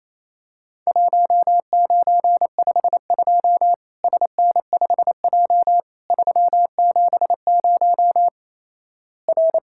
Morse code, heard as a keyed tone, about 27 wpm, 700 Hz. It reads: 1952 SN5J 370 R
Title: Worked All Europe DX Contest - QTC Exchange - 28wpm